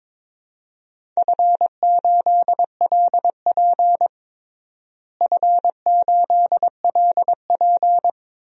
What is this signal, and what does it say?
Morse code, heard as a keyed tone, about 22 wpm, 700 Hz. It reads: F8LP F8LP
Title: DXCC-100 Call Signs - 4 Characters - Repeated 2x - 22wpm